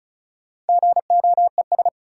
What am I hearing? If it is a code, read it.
Morse code, heard as a keyed tone, about 36 wpm, 700 Hz. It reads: GOES
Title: Sentences from Top 1000 Words - Review Part 5 - New Word Drill - 35wpm